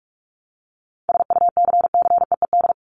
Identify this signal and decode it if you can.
Morse code, heard as a keyed tone, about 44 wpm, 700 Hz. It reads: SUCCEED
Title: Top 1501-1750 Words 45wpm